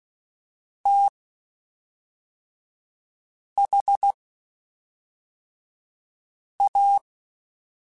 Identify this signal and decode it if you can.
Morse code, heard as a keyed tone, about 16 wpm, spaced as for 3 wpm, 800 Hz. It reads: THA
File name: Quotes Spaced / Quotes Spaced-mp3s / StandFor sp @5